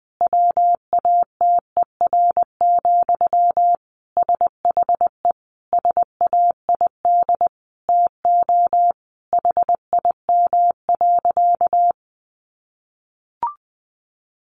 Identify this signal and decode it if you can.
Morse code, heard as a keyed tone, about 20 wpm, 700 Hz. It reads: WATER, SHE SAID TO HIM.
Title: Sentences from Top 800 Words 20wpm